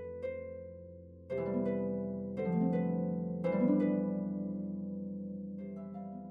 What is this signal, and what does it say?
There is no code.